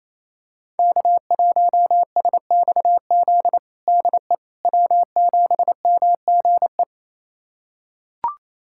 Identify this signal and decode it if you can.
Morse code, heard as a keyed tone, about 28 wpm, 700 Hz. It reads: K1SXZ DE W7MGE
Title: USA Callsign DE Callsign - 5 Characters - 28wpm